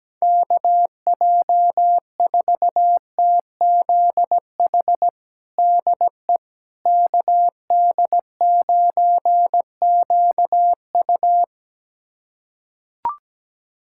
Morse code, heard as a keyed tone, about 17 wpm, 700 Hz. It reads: KJ4TZH DE KD9QU
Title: USA Callsign DE Callsign 17wpm